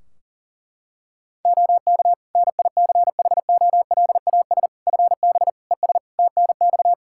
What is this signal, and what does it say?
Morse code, heard as a keyed tone, about 37 wpm, 700 Hz. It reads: OK NICHOLAS FB ES TNX
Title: Practice QSO - Exchange 2 - Info - 40wpm